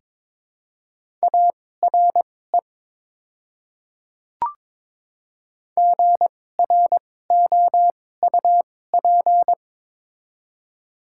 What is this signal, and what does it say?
Morse code, heard as a keyed tone, about 22 wpm, spaced as for 16 wpm, 700 Hz. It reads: ARE GROUP
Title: 2x Character Spacing - Top 300 Words - 22wpm